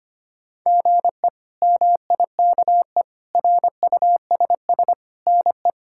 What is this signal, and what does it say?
Morse code, heard as a keyed tone, about 25 wpm, 700 Hz. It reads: GE MIKE RUSS NE
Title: K1USN SST QSO Practice - 25wpm